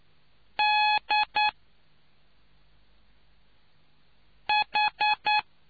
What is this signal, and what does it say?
Morse code, heard as a keyed tone, about 9 wpm, 840 Hz. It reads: D H